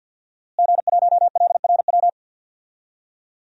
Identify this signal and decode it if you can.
Morse code, heard as a keyed tone, about 49 wpm, 700 Hz. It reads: G1LRW
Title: DXCC-100 Call Signs - 5 Characters - Repeated 3x - 50wpm